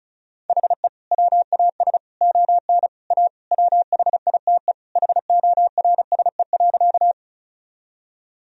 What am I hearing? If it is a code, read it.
Morse code, heard as a keyed tone, about 36 wpm, 700 Hz. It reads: HE WAS ON A WHITE HORSE.
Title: Sentences from Top 1000 Words - Review Part 1 - 35wpm